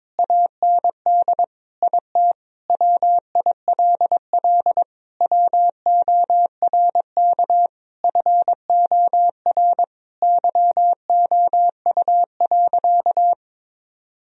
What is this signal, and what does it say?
Morse code, heard as a keyed tone, about 22 wpm, 700 Hz. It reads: AND IT WILL WORK FOR YOU.
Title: Sentences from Top 1000 Words - Review Part 2 - 22wpm